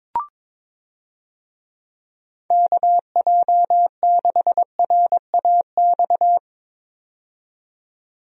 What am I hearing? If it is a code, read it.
Morse code, heard as a keyed tone, about 22 wpm, 700 Hz. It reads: KJ6RAX
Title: USA Call Signs - 6 Characters - 22wpm